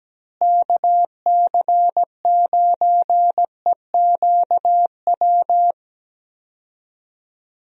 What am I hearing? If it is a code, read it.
Morse code, heard as a keyed tone, about 17 wpm, 700 Hz. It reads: KC9EQW
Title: USA Call Signs - 6 Characters - Repeated 2x - 17wpm